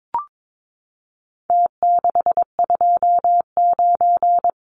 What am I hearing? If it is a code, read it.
Morse code, heard as a keyed tone, about 22 wpm, 700 Hz. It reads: T629
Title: Numbers - Contest Style - 4-Digit 22wpm